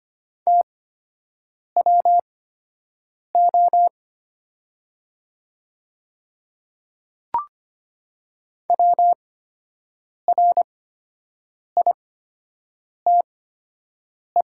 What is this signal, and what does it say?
Morse code, heard as a keyed tone, about 25 wpm, spaced as for 7 wpm, 700 Hz. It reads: TWO WRITE
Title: 8x Character Spacing - Top 100 Words - 25wpm